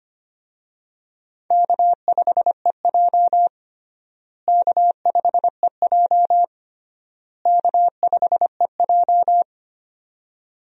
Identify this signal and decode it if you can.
Morse code, heard as a keyed tone, about 25 wpm, 700 Hz. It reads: K5EJ K5EJ K5EJ
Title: USA Call Signs - 4 Characters - Repeated 3x - 25wpm